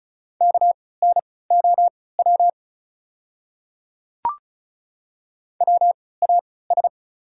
Morse code, heard as a keyed tone, about 36 wpm, spaced as for 20 wpm, 690 Hz. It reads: KNOW WAS
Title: 3x Character Spacing - Top 200 Words - 35wpm